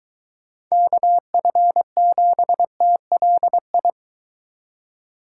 Morse code, heard as a keyed tone, about 21 wpm, 700 Hz. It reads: KF7TLI